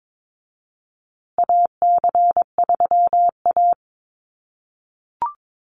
Morse code, heard as a keyed tone, about 22 wpm, 700 Hz. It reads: AC3A
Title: USA Call Signs Repeated 3x 22wpm